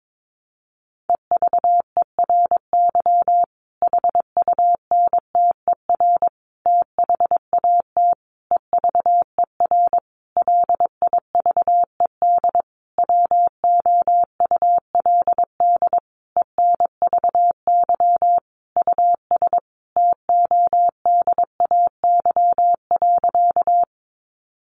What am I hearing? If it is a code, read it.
Morse code, heard as a keyed tone, about 22 wpm, 700 Hz. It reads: EVERY HUNTER THAT EVER LIVED WOULD ENVY US TODAY.
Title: A Sound Of Thunder 22wpm